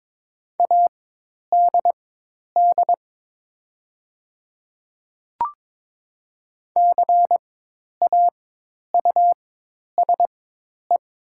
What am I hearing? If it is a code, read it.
Morse code, heard as a keyed tone, about 22 wpm, spaced as for 10 wpm, 700 Hz. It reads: ADD CAUSE
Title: 4x Character Spacing - Top 300 Words - 22wpm